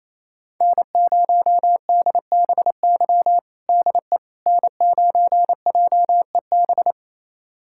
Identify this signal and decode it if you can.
Morse code, heard as a keyed tone, about 28 wpm, 700 Hz. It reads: N0DBY DE N9JEB
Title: USA Callsign DE Callsign 28wpm